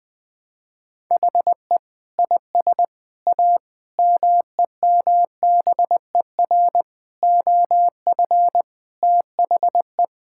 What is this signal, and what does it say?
Morse code, heard as a keyed tone, about 20 wpm, 700 Hz. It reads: HE IS A MEMBER OF THE
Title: Sentences with 3-Word Phrases - Spoken Beginning - 20wpm